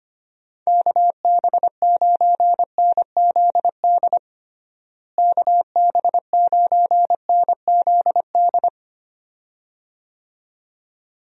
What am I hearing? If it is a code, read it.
Morse code, heard as a keyed tone, about 25 wpm, 700 Hz. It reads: KB9NZD KB9NZD
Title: USA Call Signs - 6 Characters - Repeated 2x - 25wpm